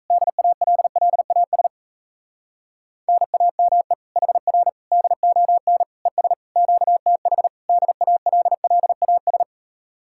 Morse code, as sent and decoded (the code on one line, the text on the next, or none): -.. .- .-.. .-.. .- ... / -. .- -- . / .... .-. / -.. --- -. / . ... / --.- - .... / -.. .- .-.. .-.. .- ...
DALLAS NAME HR DON ES QTH DALLAS